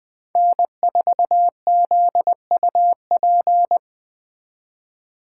Morse code, heard as a keyed tone, about 21 wpm, 700 Hz. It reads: N4ZUP
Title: Speed Racing - USA 1x3 Call Signs 17wpm